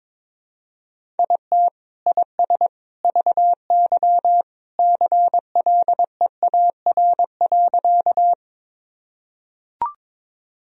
Morse code, heard as a keyed tone, about 21 wpm, 700 Hz. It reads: IT IS VY CLEAR.
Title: Sentences from Top 500 Words and Abbreviations 22wpm